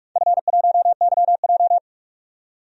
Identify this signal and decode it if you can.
Morse code, heard as a keyed tone, about 41 wpm, 700 Hz. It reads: F1YJ
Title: DXCC-100 Call Signs - 4 Characters - 45wpm